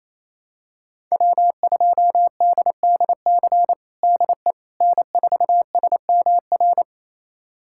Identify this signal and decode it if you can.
Morse code, heard as a keyed tone, about 28 wpm, 700 Hz. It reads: W2DDC DE N4SMR
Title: USA Callsign DE Callsign - 5 Characters - 28wpm